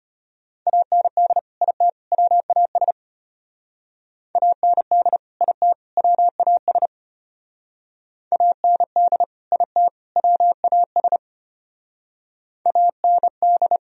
Morse code, heard as a keyed tone, about 33 wpm, 700 Hz. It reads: AND IT WAS AND IT WAS AND IT WAS AND